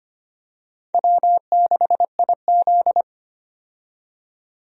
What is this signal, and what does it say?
Morse code, heard as a keyed tone, about 25 wpm, 700 Hz. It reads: W6IZ
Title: USA Call Signs - 4 Characters - 25wpm